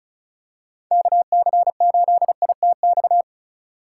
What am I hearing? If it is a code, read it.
Morse code, heard as a keyed tone, about 35 wpm, 700 Hz. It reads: KC8ITX
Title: USA Call Signs - 6 Characters - 3x Character Spacing - 35wpm